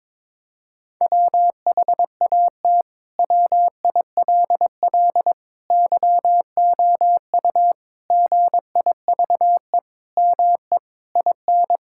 Morse code, heard as a keyed tone, about 22 wpm, 700 Hz. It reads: WHAT WILL YOU GIVE ME IN
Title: Sentences from Top 1000 Words - Review Part 6 - 22wpm